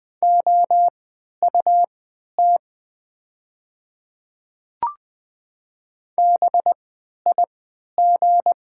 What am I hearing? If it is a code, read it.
Morse code, heard as a keyed tone, about 20 wpm, spaced as for 11 wpm, 700 Hz. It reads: OUT BIG
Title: 3x Character Spacing - Top 200 Words - 20wpm